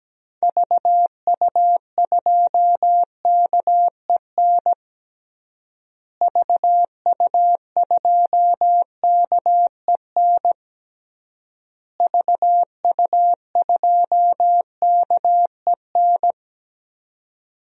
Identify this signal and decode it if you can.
Morse code, heard as a keyed tone, about 17 wpm, 700 Hz. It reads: VU2KEN VU2KEN VU2KEN
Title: DXCC-100 Call Signs - 6 Characters - Repeated 3x - 17wpm